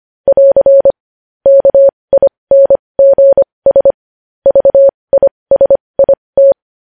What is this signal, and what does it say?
Morse code, heard as a keyed tone, about 25 wpm, spaced as for 20 wpm, 550 Hz. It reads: <AR> KINGS VISIT
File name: podcast / News headlines-1x-25x20